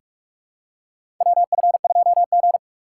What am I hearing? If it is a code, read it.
Morse code, heard as a keyed tone, about 46 wpm, 700 Hz. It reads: WF2G
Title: USA Call Signs - 2x Character Spacing - 45wpm